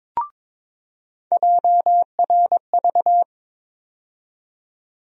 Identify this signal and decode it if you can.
Morse code, heard as a keyed tone, about 22 wpm, 700 Hz. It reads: JRV
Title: USA Call Sign Postfixes - 3 Letters 22wpm